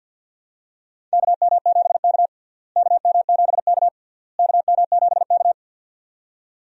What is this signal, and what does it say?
Morse code, heard as a keyed tone, about 52 wpm, 700 Hz. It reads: KM7K KM7K KM7K
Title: USA Call Signs - 4 Characters - Repeated 3x - 50wpm